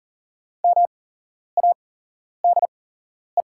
Morse code, heard as a keyed tone, about 40 wpm, spaced as for 11 wpm, 700 Hz. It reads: MADE
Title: 8x Character Spacing - Top 200 Words - 40wpm